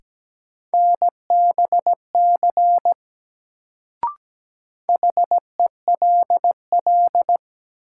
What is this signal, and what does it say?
Morse code, heard as a keyed tone, about 17 wpm, 700 Hz. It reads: NBC HELL
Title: Sets of 5 Words Encore - New Word Drill - 17wpm